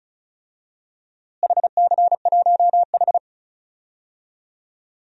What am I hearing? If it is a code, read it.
Morse code, heard as a keyed tone, about 36 wpm, 700 Hz. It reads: HC1H
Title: DXCC-100 Call Signs - 4 Characters - Repeated 2x - 35wpm